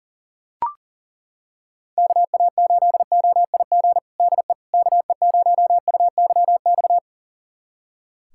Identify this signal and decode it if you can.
Morse code, heard as a keyed tone, about 37 wpm, 700 Hz. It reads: KA8OIG DE KE0UYX
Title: USA Callsign DE Callsign - 6 Characters - 40wpm